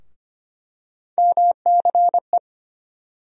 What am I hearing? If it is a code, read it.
Morse code, heard as a keyed tone, about 25 wpm, 700 Hz. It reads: MCE
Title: USA Call Sign Postfixes - 3 Letters 25wpm